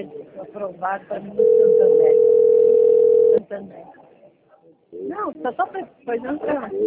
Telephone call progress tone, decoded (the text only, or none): ringback tone